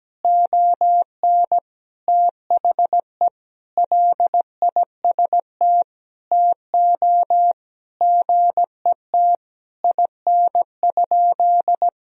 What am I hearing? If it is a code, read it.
Morse code, heard as a keyed tone, about 17 wpm, 700 Hz. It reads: ON THE LIST TO GET IN?
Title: Sentences from Top 1000 Words - Review Part 2 - 17wpm